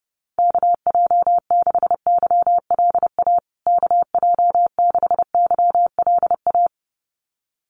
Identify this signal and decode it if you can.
Morse code, heard as a keyed tone, about 30 wpm, 700 Hz. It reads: KJ6YLA KJ6YLA
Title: USA Call Signs Repeated 2x 30wpm